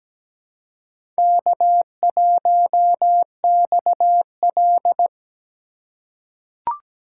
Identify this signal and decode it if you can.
Morse code, heard as a keyed tone, about 17 wpm, 700 Hz. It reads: K1XL
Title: USA Call Signs - 4 Characters - 4x Character Spacing - 17wpm